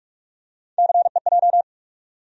Morse code, heard as a keyed tone, about 45 wpm, 700 Hz. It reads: CEJ
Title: USA Call Sign Postfixes - 3 Letters 45wpm